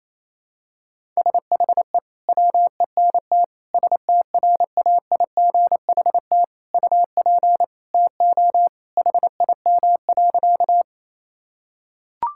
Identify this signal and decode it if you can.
Morse code, heard as a keyed tone, about 28 wpm, 700 Hz. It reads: SHE WENT STRAIGHT UP TO HIM.
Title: Sentences from Top 1000 Words - Review Part 6 - 28wpm